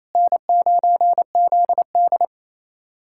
Morse code, heard as a keyed tone, about 28 wpm, 700 Hz. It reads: N9ZD
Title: USA Callsign DE Callsign - 4 Characters - 28wpm